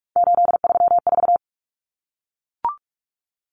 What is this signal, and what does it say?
Morse code, heard as a keyed tone, about 44 wpm, 700 Hz. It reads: 834
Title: Common Cut Numbers - 3-Digit 45wpm